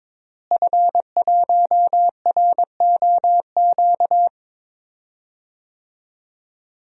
Morse code, heard as a keyed tone, about 22 wpm, 700 Hz. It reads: F1ROQ
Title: DXCC-100 Call Signs - 5 Characters - Repeated 3x - 22wpm